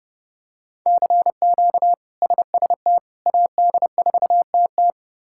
Morse code, heard as a keyed tone, about 30 wpm, 700 Hz. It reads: CQ SST AD4TT
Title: K1USN SST QSO Practice - 30wpm